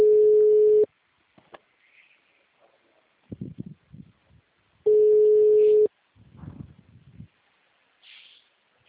Ringback tone (Europe, 425 Hz).
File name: out- -2046- - - .49910